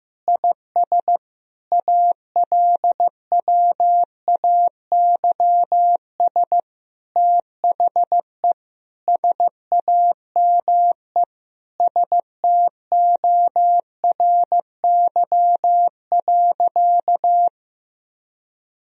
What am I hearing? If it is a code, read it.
Morse code, heard as a keyed tone, about 15 wpm, 700 Hz. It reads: IS ALWAYS THE SAME STORY.
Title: Sentences from Top 1000 Words - Review Part 1 - 15wpm